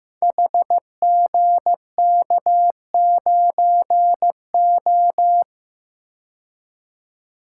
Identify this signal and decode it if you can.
Morse code, heard as a keyed tone, about 15 wpm, 700 Hz. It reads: HGK9O
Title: Code Groups - 5 Letter:Numbers - 15wpm